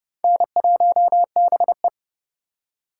Morse code, heard as a keyed tone, about 30 wpm, 700 Hz. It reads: N1BE
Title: USA Call Signs - 3x Character Spacing - 30wpm